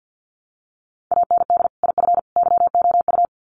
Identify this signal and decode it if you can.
Morse code, heard as a keyed tone, about 50 wpm, 700 Hz. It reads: AND IF YOU
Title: Sets of 3 Words 50wpm